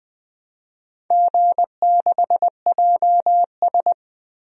Morse code, heard as a keyed tone, about 20 wpm, 700 Hz. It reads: G6JS
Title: DXCC-100 Call Signs - 4 Characters - 20wpm